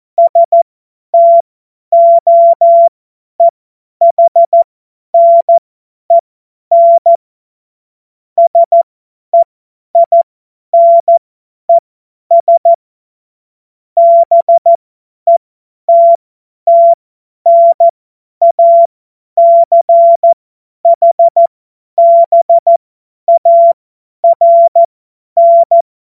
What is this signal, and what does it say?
Morse code, heard as a keyed tone, about 14 wpm, spaced as for 10 wpm, 680 Hz. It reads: STOEHNEN SEINES BETTNACHBARN
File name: Morsekurs / Witze / Witz 11. Juli 2023 50 F70